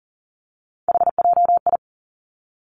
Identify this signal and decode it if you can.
Morse code, heard as a keyed tone, about 38 wpm, 700 Hz. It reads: HJI
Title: Three Challenging Letter-Number Combinations 40wpm